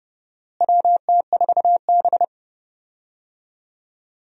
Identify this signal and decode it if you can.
Morse code, heard as a keyed tone, about 30 wpm, 700 Hz. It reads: WT4B
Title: USA Call Signs - 4 Characters - Repeated 3x - 30wpm